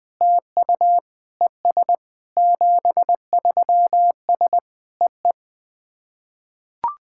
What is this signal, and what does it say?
Morse code, heard as a keyed tone, about 21 wpm, 700 Hz. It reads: TU ES 73S EE
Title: SOTA QSO Practice 20wpm